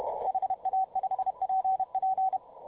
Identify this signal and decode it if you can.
Morse code, heard as a keyed tone, about 32 wpm, 750 Hz, with radio noise. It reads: HA5PP